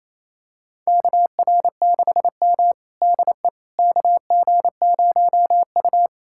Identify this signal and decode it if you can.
Morse code, heard as a keyed tone, about 28 wpm, 700 Hz. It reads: KR6M DE KG0U
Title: USA Callsign DE Callsign - 4 Characters - 28wpm